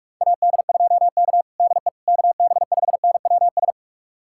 Morse code, heard as a keyed tone, about 45 wpm, 700 Hz. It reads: AD2K DE KB5NWS
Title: USA Callsign DE Callsign 45wpm